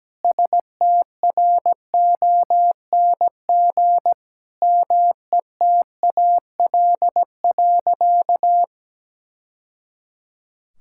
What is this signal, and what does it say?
Morse code, heard as a keyed tone, about 17 wpm, 700 Hz. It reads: STRONG METAL.
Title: Sentences from Top 1000 Words - Review Part 6 - 17wpm